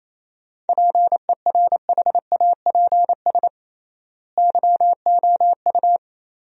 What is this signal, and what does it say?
Morse code, heard as a keyed tone, about 27 wpm, 700 Hz. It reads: PERHAPS YOU
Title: 3x Word-Spacing - Sentences from Top 500 Words - 28wpm